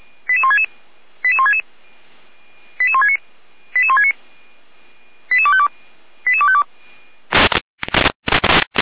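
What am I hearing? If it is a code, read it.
Five-tone selective call (ZVEI), heard as quick runs of five tones, tones of about 70 ms. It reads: 88166 88166 88169 88169 88252 88252
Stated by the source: ZVEI five-tone selective call received via SDR, recorded by Feuerwehr Orlamünde fire brigade.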